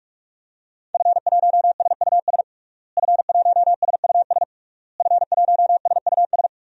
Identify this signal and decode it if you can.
Morse code, heard as a keyed tone, about 41 wpm, 700 Hz. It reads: F1SUS F1SUS F1SUS
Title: DXCC-100 Call Signs - 5 Characters - Repeated 3x - 45wpm